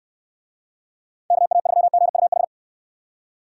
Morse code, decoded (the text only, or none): KA3GUH